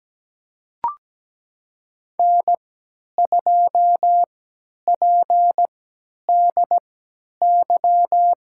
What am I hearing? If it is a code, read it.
Morse code, heard as a keyed tone, about 17 wpm, spaced as for 10 wpm, 700 Hz. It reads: N2PDY
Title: USA Call Signs - 5 Characters - 3x Character Spacing - 17wpm